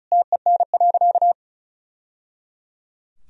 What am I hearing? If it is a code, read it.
Morse code, heard as a keyed tone, about 36 wpm, 700 Hz. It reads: TEN.
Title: Sentences from Top 1000 Words - Review Part 2 - 35wpm